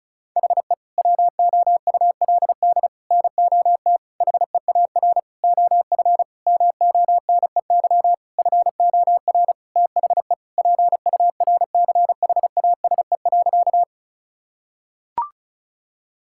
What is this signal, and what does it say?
Morse code, heard as a keyed tone, about 34 wpm, 690 Hz. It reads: HE WOULD NOT HEAR OF MONEY FOR THE PURCHASE.
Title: Sentences from Top 1000 Words - Review Part 5 - 35wpm